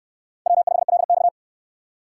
Morse code, heard as a keyed tone, about 69 wpm, 700 Hz. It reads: W5LX